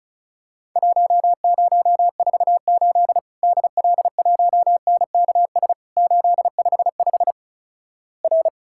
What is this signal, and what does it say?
Morse code, heard as a keyed tone, about 36 wpm, 700 Hz. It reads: 1048 DL1NKS 855 R
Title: Worked All Europe DX Contest - QTC Exchange - 35wpm